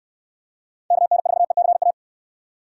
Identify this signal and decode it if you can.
Morse code, heard as a keyed tone, about 60 wpm, 700 Hz. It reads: KN4ECA